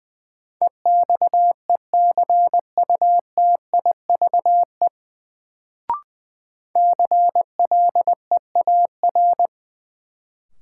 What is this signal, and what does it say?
Morse code, heard as a keyed tone, about 20 wpm, 700 Hz. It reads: EXECUTIVE CLEAR
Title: Sets of 5 Words Encore - New Word Drill - 20wpm